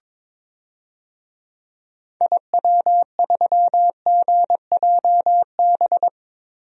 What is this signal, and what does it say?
Morse code, heard as a keyed tone, about 22 wpm, 700 Hz. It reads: IW3GJB
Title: DXCC-100 Call Signs - 6 Characters - Repeated 2x - 22wpm